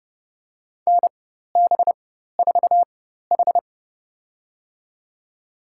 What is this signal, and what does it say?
Morse code, heard as a keyed tone, about 30 wpm, spaced as for 14 wpm, 700 Hz. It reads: NB4H
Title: USA Call Signs - 4 Characters - 4x Character Spacing - 30wpm